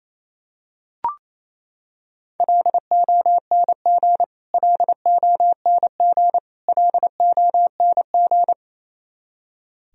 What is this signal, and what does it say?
Morse code, heard as a keyed tone, about 28 wpm, 700 Hz. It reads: LONG LONG LONG
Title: Top 100 Words Repeated 28wpm